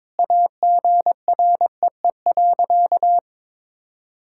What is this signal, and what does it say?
Morse code, heard as a keyed tone, about 22 wpm, 700 Hz. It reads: AGREE.
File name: practice / 22wpm / Sentences with 4-Word Phrases - 22wpm